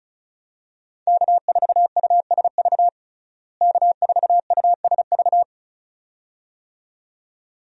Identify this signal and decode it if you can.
Morse code, heard as a keyed tone, about 36 wpm, 700 Hz. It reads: K4USV K4USV
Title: USA Call Signs - 5 Characters - Repeated 2x - 35wpm